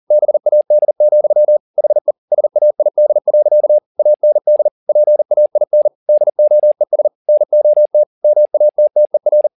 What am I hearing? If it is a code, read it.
Morse code, heard as a keyed tone, about 40 wpm, 600 Hz. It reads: BAD, HE SAID. AND PAIN DOES NOT MATTER
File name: the old man and the sea 2 40wpm